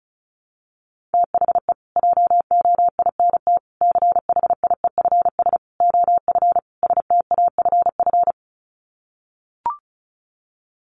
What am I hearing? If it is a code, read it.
Morse code, heard as a keyed tone, about 34 wpm, 700 Hz. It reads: THE JOINT CHIEFS OF STAFF
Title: Sets of 5 Words Encore2 35wpm